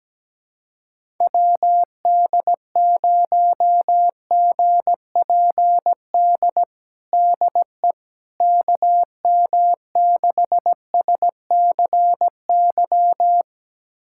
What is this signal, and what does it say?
Morse code, heard as a keyed tone, about 17 wpm, 700 Hz. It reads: WD0GPD DE KM6SCY
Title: USA Callsign DE Callsign 17wpm